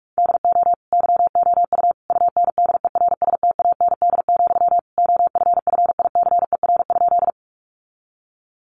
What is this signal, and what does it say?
Morse code, heard as a keyed tone, about 44 wpm, 700 Hz. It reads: DO YOU UNDERSTAND, OFFICER?
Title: Sentences from Top 1200 Words 45wpm